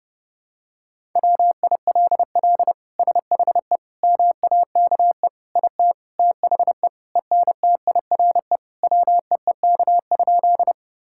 Morse code, heard as a keyed tone, about 32 wpm, 700 Hz. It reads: WILL SHE MAKE IT THE ENTIRE WEEK?
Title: Sentences from Top 1000 Words - Review Part 5 - 30wpm